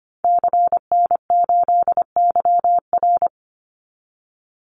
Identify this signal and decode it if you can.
Morse code, heard as a keyed tone, about 25 wpm, 700 Hz. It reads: CN8YR
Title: Call Signs Repeated 2x - 25wpm